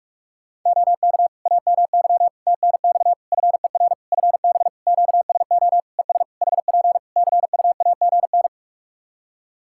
Morse code, heard as a keyed tone, about 46 wpm, 700 Hz. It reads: OK AMY TNX FER FB QSO ES HP CUAGN
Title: Practice QSO - Exchange 4 - Ending - 45wpm